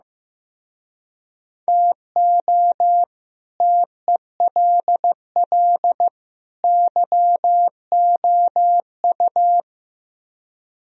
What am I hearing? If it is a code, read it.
Morse code, heard as a keyed tone, about 15 wpm, 700 Hz. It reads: TO TELL YOU
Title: Speed-Racing - Sets of 3 Words 15wpm